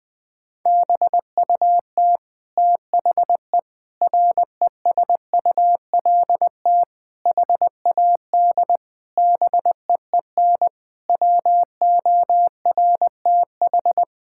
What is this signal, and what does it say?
Morse code, heard as a keyed tone, about 21 wpm, 700 Hz. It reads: BUT THE RESULT HAD BEEN WORTH
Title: Sentences from Top 1000 Words - Review Part 5 - 20wpm